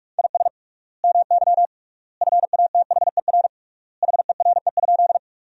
Morse code, heard as a keyed tone, about 47 wpm, 700 Hz. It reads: IS MY FATHER HERE?
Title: 3x Word-Spacing - Sentences from Top 300 Words - 45wpm